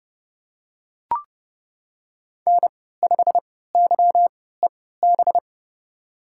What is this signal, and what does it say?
Morse code, heard as a keyed tone, about 32 wpm, spaced as for 17 wpm, 700 Hz. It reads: N5YEB
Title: USA Call Signs - 3x Character Spacing - 30wpm